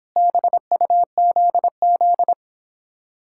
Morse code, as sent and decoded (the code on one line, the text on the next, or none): -... ..- --.. --..
BUZZ